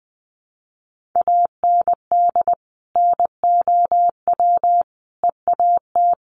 Morse code, heard as a keyed tone, about 20 wpm, 700 Hz. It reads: AND NOW EAT
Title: Sentences from Top 1500 Words 20wpm